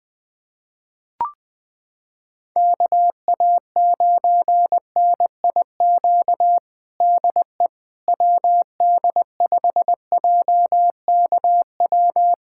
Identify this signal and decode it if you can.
Morse code, heard as a keyed tone, about 20 wpm, 700 Hz. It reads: KA9NIQ DE WD5JKW
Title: USA Callsign DE Callsign - 6 Characters - 20wpm